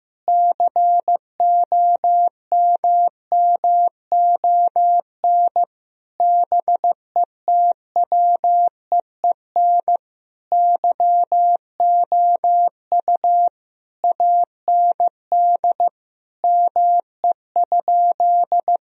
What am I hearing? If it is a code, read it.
Morse code, heard as a keyed tone, about 15 wpm, 700 Hz. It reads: COMMON BETWEEN YOU AND ME?
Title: Sentences from Top 1000 Words - Review Part 3 - 15wpm